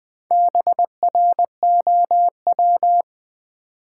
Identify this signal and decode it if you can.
Morse code, heard as a keyed tone, about 20 wpm, 700 Hz. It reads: BROW
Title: Build A Word - 1 Letter At A Time - 5-Letter Words - 20wpm